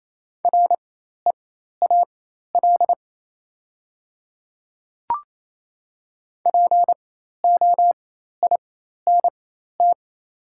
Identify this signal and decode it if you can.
Morse code, heard as a keyed tone, about 28 wpm, spaced as for 13 wpm, 690 Hz. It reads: REAL POINT
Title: 4x Character Spacing - Top 500 Words - 28wpm